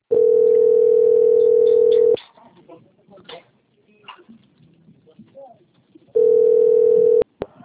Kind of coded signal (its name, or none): telephone call progress tone